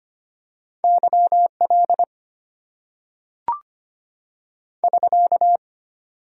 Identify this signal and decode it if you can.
Morse code, heard as a keyed tone, about 25 wpm, 700 Hz. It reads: YL <SK>
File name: Top 90 QSO Elements - 8x Character Spacing - 25wpm